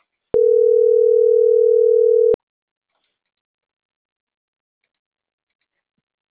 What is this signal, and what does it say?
Ringback tone (North America), 440 + 480 Hz.